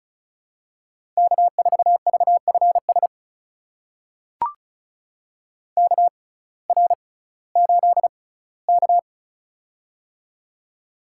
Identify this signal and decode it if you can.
Morse code, heard as a keyed tone, about 36 wpm, spaced as for 19 wpm, 700 Hz. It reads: K4VFS KR8K
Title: USA Call Signs - 6x Character Spacing - 35wpm